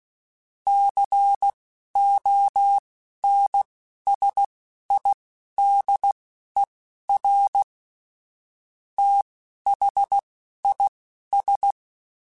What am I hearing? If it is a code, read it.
Morse code, heard as a keyed tone, about 16 wpm, spaced as for 11 wpm, 800 Hz. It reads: CONSIDER THIS
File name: Technical Subjects / Technical Subjects-mp3s / AND Gate @10